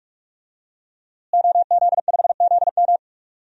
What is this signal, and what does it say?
Morse code, heard as a keyed tone, about 45 wpm, 700 Hz. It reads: OZ5ZM
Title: DXCC-100 Call Signs - 5 Characters - Repeated 3x - 45wpm